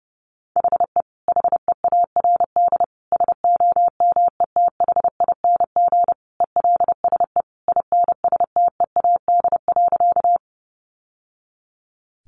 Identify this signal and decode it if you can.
Morse code, heard as a keyed tone, about 28 wpm, 700 Hz. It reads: HE HEARD SOMETHING ELSE INSTEAD.
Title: Sentences from Top 800 Words 30wpm